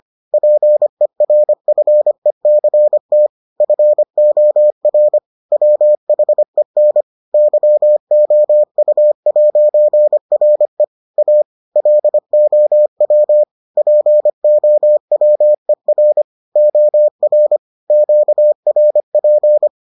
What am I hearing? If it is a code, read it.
Morse code, heard as a keyed tone, about 25 wpm, 600 Hz. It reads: PERFECT FOR WHEN YOU'RE A LOW POWER OR QRP